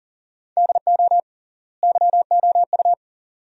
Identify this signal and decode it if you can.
Morse code, heard as a keyed tone, about 40 wpm, 700 Hz. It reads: DO YOU
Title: 3x Word-Spacing - Sentences from Top 100 Words - 40wpm